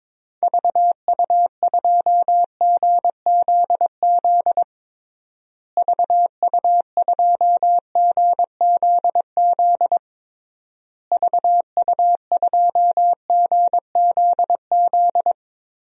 Morse code, heard as a keyed tone, about 22 wpm, 700 Hz. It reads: VU2GZZ VU2GZZ VU2GZZ
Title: DXCC-100 Call Signs - 6 Characters - Repeated 3x - 22wpm